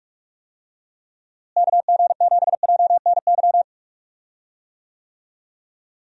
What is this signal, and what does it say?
Morse code, heard as a keyed tone, about 46 wpm, 700 Hz. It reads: KG7JNY
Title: USA Call Signs - 6 Characters - Repeated 2x - 45wpm